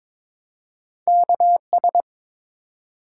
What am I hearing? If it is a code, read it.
Morse code, heard as a keyed tone, about 21 wpm, 700 Hz. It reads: KS
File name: ARRL Sections v2 - 22wpm